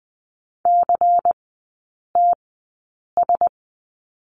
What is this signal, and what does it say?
Morse code, heard as a keyed tone, about 20 wpm, spaced as for 8 wpm, 690 Hz. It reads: CTS
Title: 159 Lesson 26e - Letter g - Three Characters 20wpm